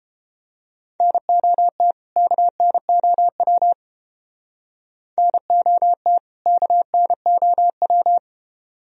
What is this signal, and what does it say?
Morse code, heard as a keyed tone, about 30 wpm, 700 Hz. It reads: NOT KNOW NOT KNOW